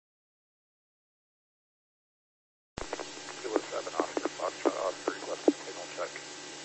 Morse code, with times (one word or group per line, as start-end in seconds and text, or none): none